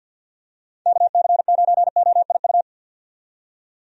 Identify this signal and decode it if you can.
Morse code, heard as a keyed tone, about 51 wpm, 700 Hz. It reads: KC9OIU
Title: USA Call Signs - 6 Characters - 50wpm